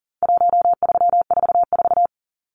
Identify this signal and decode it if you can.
Morse code, heard as a keyed tone, about 40 wpm, 700 Hz. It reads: 1344
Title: Numbers - 4-Digit 40wpm